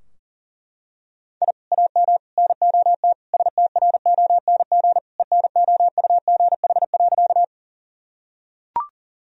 Morse code, heard as a keyed tone, about 37 wpm, 700 Hz. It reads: I AM NOT STRONG ENOUGH.
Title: Sentences from Top 1000 Words - Review Part 3 - 40wpm